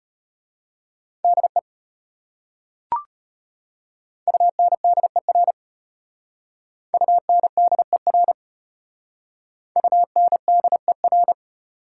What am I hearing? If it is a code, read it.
Morse code, heard as a keyed tone, about 31 wpm, 700 Hz. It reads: DE UNDER UNDER UNDER
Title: Speed-Racing Prefixes and Suffixes 25wpm